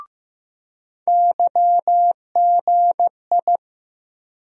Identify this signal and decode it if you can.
Morse code, heard as a keyed tone, about 15 wpm, 700 Hz. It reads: YGI
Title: USA Call Sign Postfixes - 3 Letters 15wpm